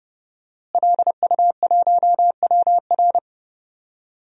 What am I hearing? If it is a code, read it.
Morse code, heard as a keyed tone, about 28 wpm, 700 Hz. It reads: LU1WR
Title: DXCC-100 Call Signs - 5 Characters - Repeated 3x - 30wpm